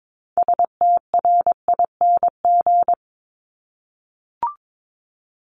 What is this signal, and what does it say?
Morse code, heard as a keyed tone, about 22 wpm, 700 Hz. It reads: STRING
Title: Multi-letter Phonemes Repeated 3x 22wpm